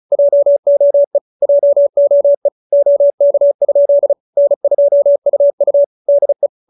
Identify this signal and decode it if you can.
Morse code, heard as a keyed tone, about 34 wpm, 570 Hz. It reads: JOE JOE OK? N2UU DE
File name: QSO313 35